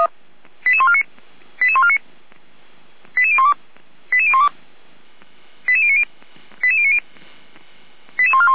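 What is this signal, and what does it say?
Five-tone selective call (ZVEI), heard as quick runs of five tones, tones of about 70 ms. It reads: 88149 88149 88012 88012 88090 88090 88141
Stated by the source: ZVEI five-tone selective call received by SDR.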